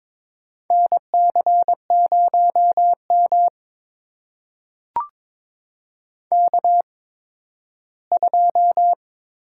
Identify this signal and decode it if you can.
Morse code, heard as a keyed tone, about 22 wpm, spaced as for 13 wpm, 700 Hz. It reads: NC0M K2
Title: USA Call Signs - 4 Characters - 8x Character Spacing - 22wpm